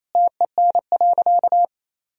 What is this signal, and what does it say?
Morse code, heard as a keyed tone, about 28 wpm, 700 Hz. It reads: TEN.
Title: Sentences from Top 1000 Words - Review Part 2 - 28wpm